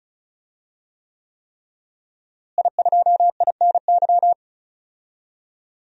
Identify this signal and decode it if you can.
Morse code, heard as a keyed tone, about 36 wpm, 700 Hz. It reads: I2INY
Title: DXCC-100 Call Signs - 5 Characters - Repeated 3x - 35wpm